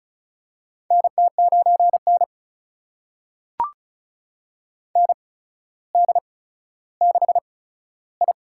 Morse code, heard as a keyed tone, about 36 wpm, spaced as for 15 wpm, 700 Hz. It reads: NT9N ND6I
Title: USA Call Signs - 4 Characters - 8x Character Spacing - 35wpm